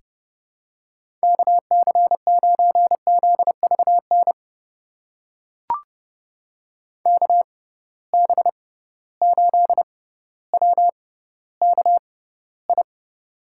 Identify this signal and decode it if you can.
Morse code, heard as a keyed tone, about 30 wpm, spaced as for 15 wpm, 700 Hz. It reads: KC9ZVN KB8WKI
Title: USA Call Signs - 6 Characters - 6x Character Spacing - 30wpm